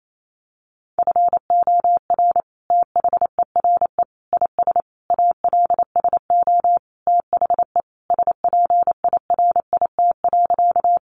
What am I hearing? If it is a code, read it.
Morse code, heard as a keyed tone, about 28 wpm, 700 Hz. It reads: FOR THERE IS ALSO THE SPIRIT.
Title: Sentences from Top 1400 Words 28wpm